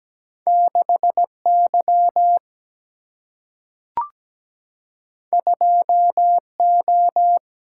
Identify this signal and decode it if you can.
Morse code, heard as a keyed tone, about 17 wpm, 700 Hz. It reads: 6Y 2O
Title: Single-Letter Single-Number 17wpm